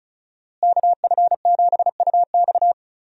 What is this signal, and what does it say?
Morse code, heard as a keyed tone, about 36 wpm, 700 Hz. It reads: KF7UX
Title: USA Call Signs - 5 Characters - Repeated 3x - 35wpm